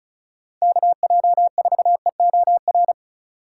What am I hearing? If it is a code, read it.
Morse code, heard as a keyed tone, about 36 wpm, 690 Hz. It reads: KJ4EOR
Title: USA Call Signs - 6 Characters - 6x Character Spacing - 35wpm